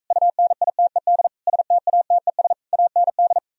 Morse code, heard as a keyed tone, about 45 wpm, 700 Hz. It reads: UNITED STATES AND